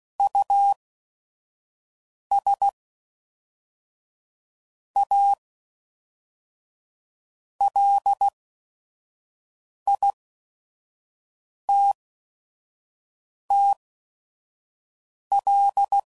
Morse code, heard as a keyed tone, about 16 wpm, spaced as for 5 wpm, 800 Hz. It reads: US A LITTL
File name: CW Code Groups @5WPM / QSOs / QSO WD0FBT